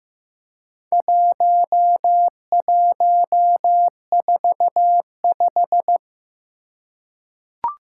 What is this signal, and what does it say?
Morse code, heard as a keyed tone, about 15 wpm, 700 Hz. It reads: 1145
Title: Four Challenging Letter-Number Combinations 15wpm